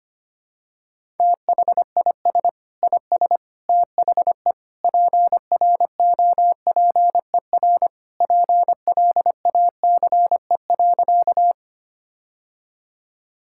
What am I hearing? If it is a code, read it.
Morse code, heard as a keyed tone, about 25 wpm, 700 Hz. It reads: THIS IS THE PROPER PLACE.
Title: Sentences from Top 1000 Words - Review Part 5 - 25wpm